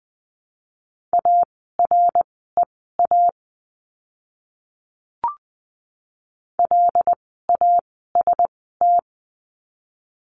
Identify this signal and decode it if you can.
Morse code, heard as a keyed tone, about 20 wpm, spaced as for 15 wpm, 700 Hz. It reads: AREA LAST
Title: 108 Lesson 18f - Letter m - Words 20wpm